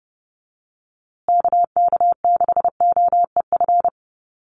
Morse code, heard as a keyed tone, about 30 wpm, 700 Hz. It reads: KK6OEF
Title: USA Call Signs Repeated 3x 30wpm